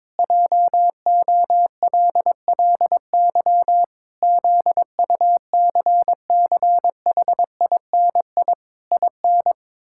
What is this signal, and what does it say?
Morse code, heard as a keyed tone, about 22 wpm, 700 Hz. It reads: JOLLY ZUCCHINI IN
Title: Pangram Challenge v2 - 22wpm